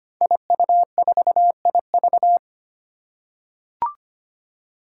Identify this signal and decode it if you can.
Morse code, heard as a keyed tone, about 25 wpm, 700 Hz. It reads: IU4IV
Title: Five Challenging Letter-Number Combinations 25wpm